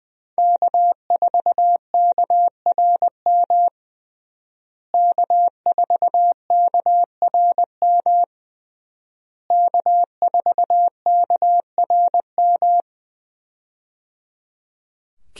Morse code, heard as a keyed tone, about 20 wpm, 700 Hz. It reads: K4KRM K4KRM K4KRM
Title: USA Call Signs - 5 Characters - Repeated 3x - 20wpm